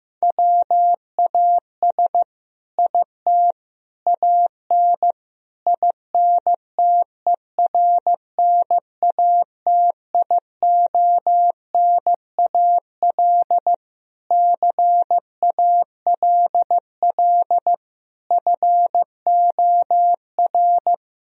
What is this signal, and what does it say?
Morse code, heard as a keyed tone, about 15 wpm, 700 Hz. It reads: WAS IT AN INTERNATIONAL CALL FOR
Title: Sentences from Top 1000 Words - Review Part 4 - 15wpm